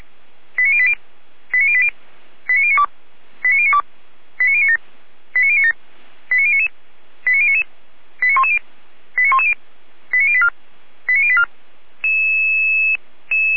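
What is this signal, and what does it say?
Five-tone selective call (ZVEI), heard as quick runs of five tones, tones of about 70 ms. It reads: 89080 89080 89092 89092 89097 89097 89099 89099 89110 89110 89084 89084
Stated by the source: ZVEI five-tone call 89073 received by SDR, logged by feuerwehr-orlamuende.org fire brigade server.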